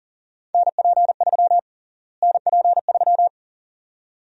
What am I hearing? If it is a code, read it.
Morse code, heard as a keyed tone, about 37 wpm, 700 Hz. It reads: NP3 NP3
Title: USA Call Sign Prefixes - Repeated 2x - 40wpm